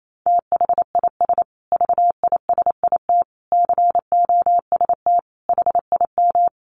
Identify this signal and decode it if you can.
Morse code, heard as a keyed tone, about 28 wpm, 700 Hz. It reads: THIS VISIT COST HIM
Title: Sentences from Top 1400 Words 28wpm